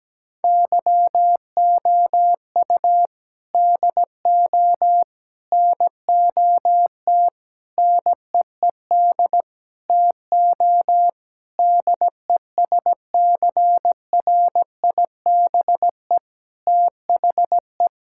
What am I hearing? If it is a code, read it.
Morse code, heard as a keyed tone, about 17 wpm, 700 Hz. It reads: YOU DO NOT NEED TO DESCRIBE THE
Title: Sentences from Top 1600 Words 17wpm